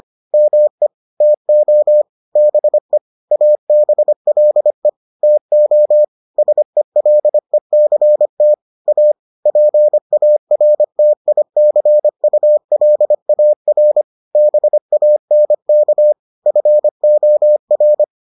Morse code, heard as a keyed tone, about 25 wpm, 600 Hz. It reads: ME TO BE ABLE TO SELECT A PARTICULAR BANK FOR